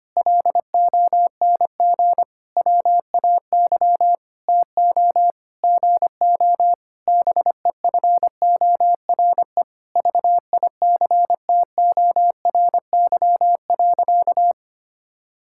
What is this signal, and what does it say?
Morse code, heard as a keyed tone, about 25 wpm, 700 Hz. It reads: LONG WAY TO GO BEFORE VICTORY.
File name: practice / 25wpm / Sentences with 9-Word Phrases - 25wpm